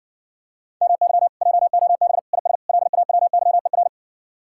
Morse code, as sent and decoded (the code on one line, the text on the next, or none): -- -.-- / .--- --- -... / .. ... / .-.. .- .-- -.-- . .-.
MY JOB IS LAWYER